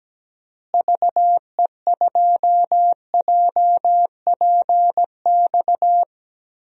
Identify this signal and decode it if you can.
Morse code, heard as a keyed tone, about 17 wpm, 700 Hz. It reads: VE2JPX
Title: DXCC-100 Call Signs - 6 Characters - Repeated 2x - 17wpm